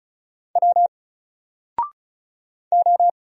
Morse code, heard as a keyed tone, about 36 wpm, 700 Hz. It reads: W O
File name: Single-Letter-Number-35wpm